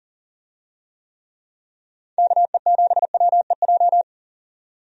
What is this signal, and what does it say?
Morse code, heard as a keyed tone, about 40 wpm, 700 Hz. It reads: KE7WEJ
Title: USA Call Signs - 6 Characters - 40wpm